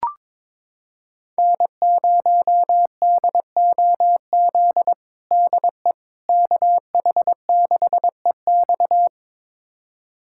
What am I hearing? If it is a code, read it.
Morse code, heard as a keyed tone, about 22 wpm, 700 Hz. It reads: N0DOZ DE KH6EX
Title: USA Callsign DE Callsign - 5 Characters - 22wpm